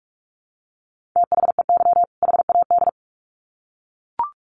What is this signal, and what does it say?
Morse code, heard as a keyed tone, about 44 wpm, 700 Hz. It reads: THEY HAD
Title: Sets of 2 Words 45wpm